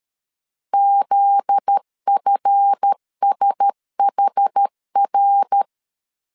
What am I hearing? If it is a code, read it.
Morse code, heard as a keyed tone, about 13 wpm, 800 Hz. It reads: ZFSHR
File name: CW / DET / 1 9 1500C S A 12wpm 3800